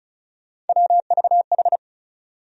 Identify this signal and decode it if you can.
Morse code, heard as a keyed tone, about 36 wpm, 700 Hz. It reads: WVH
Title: USA Call Sign Postfixes - 3 Letters 35wpm